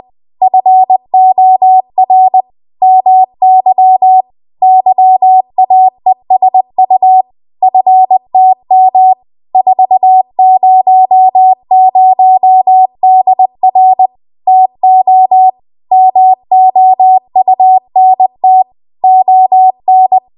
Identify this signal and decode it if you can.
Morse code, heard as a keyed tone, about 20 wpm, 750 Hz. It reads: FOR MY YAESU FTM 400DR TO MOUNT ON
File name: Archive / 20 WPM / 20WPM